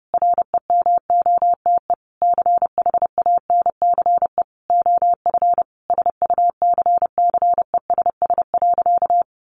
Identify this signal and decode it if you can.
Morse code, heard as a keyed tone, about 30 wpm, 700 Hz. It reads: REMOTE CHANCE OF SUCCESS.
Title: Sentences from Top 1300 Words 30wpm